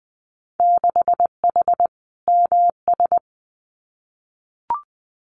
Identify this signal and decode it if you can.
Morse code, heard as a keyed tone, about 20 wpm, 700 Hz. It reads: 6H MS
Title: Winter Field Day Exchanges Encore 20wpm